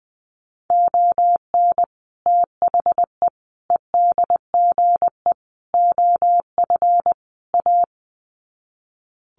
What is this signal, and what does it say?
Morse code, heard as a keyed tone, about 20 wpm, 700 Hz. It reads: ON THE EDGE OF A
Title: Sets of 5 Words Encore2 20wpm